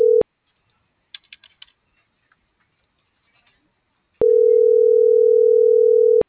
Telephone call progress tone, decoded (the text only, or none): ringback tone